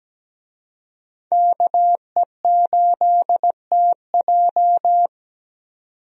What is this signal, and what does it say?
Morse code, heard as a keyed tone, about 17 wpm, 700 Hz. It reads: KE8TJ
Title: USA Call Signs - 5 Characters - 17wpm